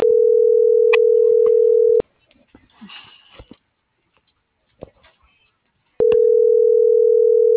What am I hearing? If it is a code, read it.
Ringback tone (North America), 440 + 480 Hz.